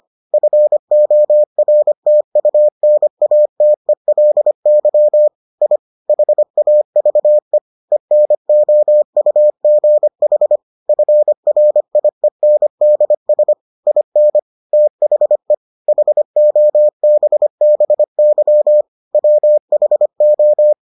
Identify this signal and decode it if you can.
Morse code, heard as a keyed tone, about 25 wpm, 600 Hz. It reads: FORTUNATELY I HAVE ENOUGH FRIENDS IN THE HOBBY WHO